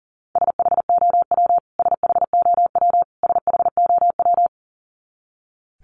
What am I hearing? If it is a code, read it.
Morse code, heard as a keyed tone, about 40 wpm, 700 Hz. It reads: SHOW SHOW SHOW
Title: Top 300 Words Repeated 40wpm